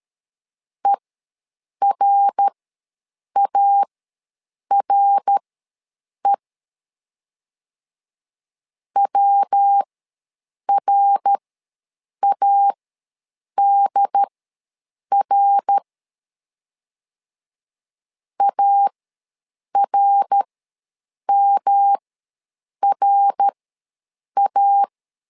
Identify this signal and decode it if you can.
Morse code, heard as a keyed tone, about 13 wpm, spaced as for 7 wpm, 800 Hz. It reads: ERARE WRADR ARMRA
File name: LEZIONI / 3 MWRO / 3 MWR ws 12wpm 1809